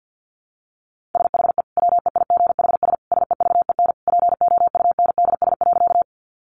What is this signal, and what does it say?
Morse code, heard as a keyed tone, about 50 wpm, 700 Hz. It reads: SHE WEIGHS SEVEN POUNDS.
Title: Easy Phrases 50wpm